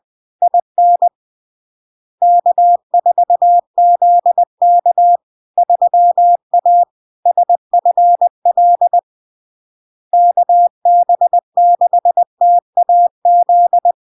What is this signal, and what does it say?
Morse code, heard as a keyed tone, about 20 wpm, 700 Hz. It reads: IN K4ZK 3A SFL KB6TAZ
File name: LARK-FD-2022-Log-in-CW-20wpm-700-Hz